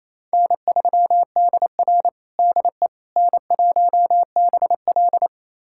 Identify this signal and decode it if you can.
Morse code, heard as a keyed tone, about 28 wpm, 700 Hz. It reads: N3DR DE N1BL
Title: USA Callsign DE Callsign - 4 Characters - 28wpm